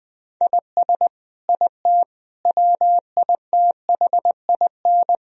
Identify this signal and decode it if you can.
Morse code, heard as a keyed tone, about 20 wpm, 700 Hz. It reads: IS IT WITHIN
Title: Sentences from Top 1000 Words - Review Part 4 - 20wpm